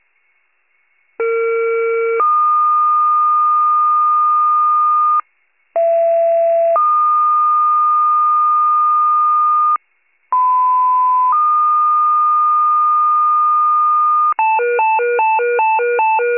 Two-tone page, heard as a one-second tone, then a three-second tone, about 460 then 1220 Hz, about 670 then 1220 Hz, about 1010 then 1220 Hz. It is heard from 1.2 to 14.3 s.
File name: pager / 39 RESCUE / wcalert